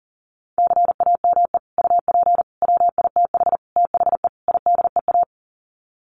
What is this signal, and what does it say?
Morse code, heard as a keyed tone, about 40 wpm, 700 Hz. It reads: CAME UP WITH THE IDEA
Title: Sets of 5 Words Encore2 40wpm